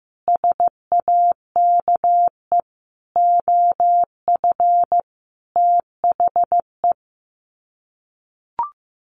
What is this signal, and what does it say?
Morse code, heard as a keyed tone, about 15 wpm, 700 Hz. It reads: SAKE OF THE
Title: Sets of 5 Words Encore 15wpm